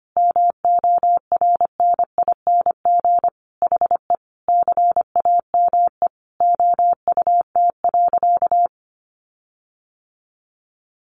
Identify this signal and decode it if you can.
Morse code, heard as a keyed tone, about 25 wpm, 700 Hz. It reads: MORNING HE CAME OUT.
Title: Sentences from Top 1300 Words 25wpm